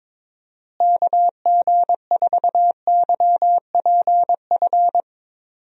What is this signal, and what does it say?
Morse code, heard as a keyed tone, about 22 wpm, 700 Hz. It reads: KG4YPF
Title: USA Call Signs - 6 Characters - Repeated 3x - 22wpm